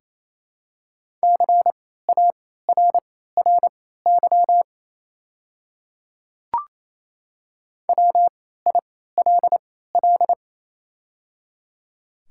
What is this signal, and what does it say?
Morse code, heard as a keyed tone, about 27 wpm, spaced as for 16 wpm, 700 Hz. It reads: CARRY WILL
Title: 3x Character Spacing - Top 300 Words - 28wpm